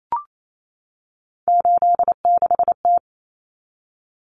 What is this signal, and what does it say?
Morse code, heard as a keyed tone, about 28 wpm, 700 Hz. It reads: 86T
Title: Common Cut Numbers - 3-Digit 28wpm